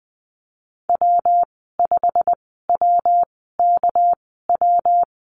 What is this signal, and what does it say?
Morse code, heard as a keyed tone, about 20 wpm, spaced as for 14 wpm, 700 Hz. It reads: W5WKW
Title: 245 Lesson 39g - Letter z - Call Signs 20wpm